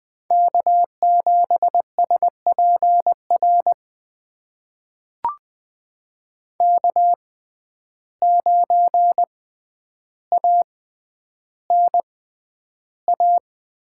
Morse code, heard as a keyed tone, about 20 wpm, spaced as for 10 wpm, 700 Hz. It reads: K7SPR K9ANA
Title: USA Call Signs - 5 Characters - 6x Character Spacing - 20wpm